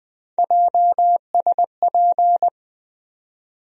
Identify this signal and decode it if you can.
Morse code, heard as a keyed tone, about 20 wpm, 700 Hz. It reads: JSP
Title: USA Call Sign Postfixes - 3 Letters 20wpm